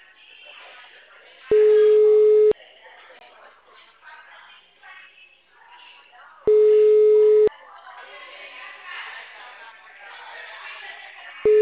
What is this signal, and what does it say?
Ringback tone (Europe, 425 Hz), with speech.